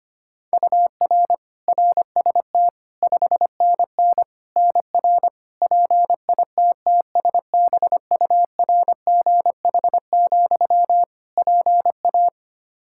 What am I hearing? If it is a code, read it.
Morse code, heard as a keyed tone, about 25 wpm, 700 Hz. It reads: UR RST 5NN NR PITTSBURGH, PA
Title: Practice QSO - Exchange 1 - Beginning - 25wpm